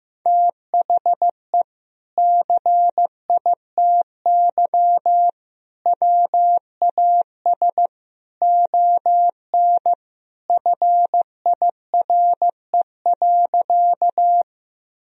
Morse code, heard as a keyed tone, about 16 wpm, 700 Hz. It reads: THE CITY WAS ON FIRE.
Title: Sentences from Top 1000 Words - Review Part 1 - 15wpm